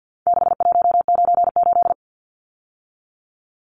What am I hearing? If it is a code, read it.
Morse code, heard as a keyed tone, about 50 wpm, 700 Hz. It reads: 6198
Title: Numbers - 4-Digit 50wpm